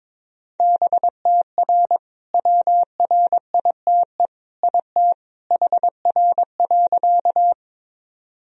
Morse code, heard as a keyed tone, about 22 wpm, 700 Hz. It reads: BTR WRITE IT HR.
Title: Sentences from Top 400 Words and Abbreviations 22wpm